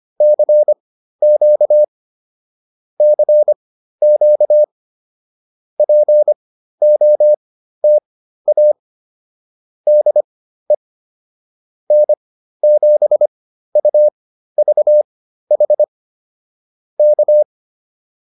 Morse code, heard as a keyed tone, about 25 wpm, spaced as for 13 wpm, 600 Hz. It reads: CQ CQ POTA DE N7UVH K